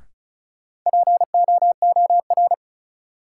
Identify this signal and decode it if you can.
Morse code, heard as a keyed tone, about 36 wpm, 700 Hz. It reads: POOR
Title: Sentences from Top 1000 Words - New Word Drill - 35wpm